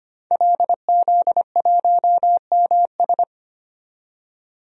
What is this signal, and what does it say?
Morse code, heard as a keyed tone, about 25 wpm, 700 Hz. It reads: LZ1MS
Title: DXCC-100 Call Signs - 5 Characters - 25wpm